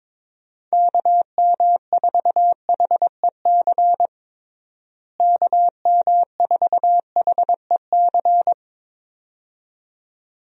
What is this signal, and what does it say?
Morse code, heard as a keyed tone, about 23 wpm, 700 Hz. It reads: KM4HEC KM4HEC
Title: USA Call Signs - 6 Characters - Repeated 2x - 22wpm